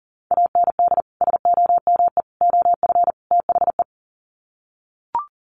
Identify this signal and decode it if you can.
Morse code, heard as a keyed tone, about 40 wpm, 700 Hz. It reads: AND SOME OF THE
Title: Sets of 4 Words 40wpm